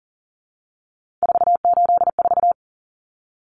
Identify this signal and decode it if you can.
Morse code, heard as a keyed tone, about 40 wpm, 700 Hz. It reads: 484
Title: Numbers - 3-Digit 40wpm